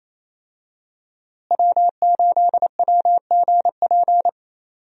Morse code, heard as a keyed tone, about 28 wpm, 700 Hz. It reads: W8WGP
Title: USA Call Signs - 5 Characters - Repeated 3x - 28wpm